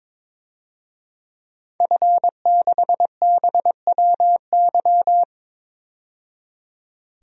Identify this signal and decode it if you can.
Morse code, heard as a keyed tone, about 22 wpm, 700 Hz. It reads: F6BWY
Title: DXCC-100 Call Signs - 5 Characters - 22wpm